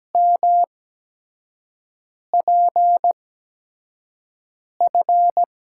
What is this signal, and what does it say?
Morse code, heard as a keyed tone, about 17 wpm, spaced as for 5 wpm, 700 Hz. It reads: MPF
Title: USA Call Signs - 6 Characters - 8x Character Spacing - 17wpm